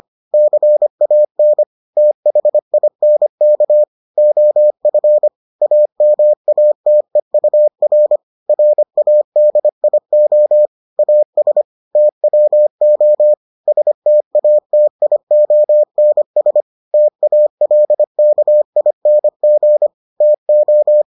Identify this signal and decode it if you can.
Morse code, heard as a keyed tone, about 25 wpm, 600 Hz. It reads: CAN THINK OF AMATEUR RADIO AS TWO STATIONS TALKING TO